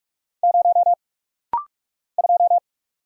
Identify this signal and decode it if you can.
Morse code, heard as a keyed tone, about 43 wpm, 700 Hz. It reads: 0 2
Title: Single Letter-Number - Rapid-Fire 45wpm